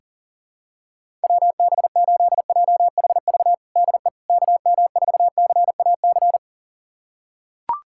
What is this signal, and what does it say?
Morse code, heard as a keyed tone, about 40 wpm, 700 Hz. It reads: WB8JHV DE KM4CAC
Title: USA Callsign DE Callsign 40wpm